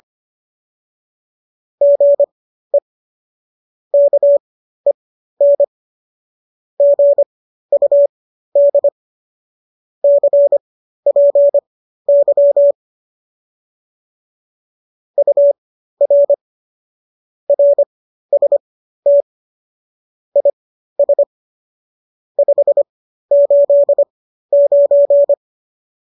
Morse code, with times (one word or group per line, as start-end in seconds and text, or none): 1.8-2.8 GE
3.9-5.6 KEN
6.8-8.9 GUD
10.0-12.7 CPY
15.2-16.3 UR
17.5-19.2 RST
20.4-21.2 IS
22.4-25.3 589